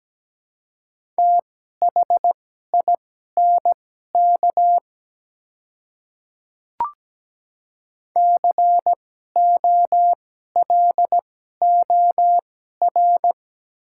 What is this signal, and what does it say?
Morse code, heard as a keyed tone, about 17 wpm, spaced as for 12 wpm, 700 Hz. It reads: THINK COLOR
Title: 2x Character Spacing - Top 300 Words - 17wpm